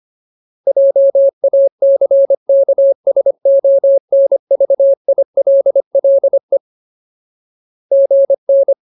Morse code, heard as a keyed tone, about 25 wpm, 550 Hz. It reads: JACKSONVILLE GN
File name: Ultimate Word List / 25WPM Ultimate